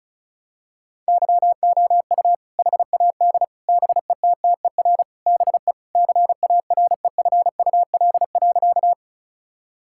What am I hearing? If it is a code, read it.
Morse code, heard as a keyed tone, about 36 wpm, 700 Hz. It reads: YOU HAD BETTER BE CAREFUL.
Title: Sentences from Top 1600 Words 35wpm